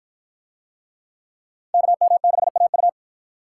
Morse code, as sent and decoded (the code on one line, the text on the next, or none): -.- -- -.... .- ..-
KM6AU